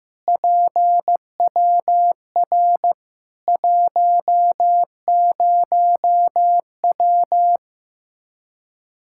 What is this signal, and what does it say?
Morse code, heard as a keyed tone, about 15 wpm, 700 Hz. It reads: PWR 10W
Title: Practice QSO - Exchange 2 - Info - 15wpm